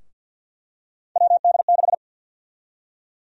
Morse code, heard as a keyed tone, about 46 wpm, 700 Hz. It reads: WD6
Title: USA Call Sign Prefixes - Repeated 2x - 50wpm